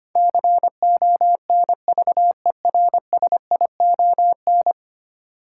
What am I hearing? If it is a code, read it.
Morse code, heard as a keyed tone, about 25 wpm, 700 Hz. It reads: CONVERSION
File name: Ten Letter Words - 25wpm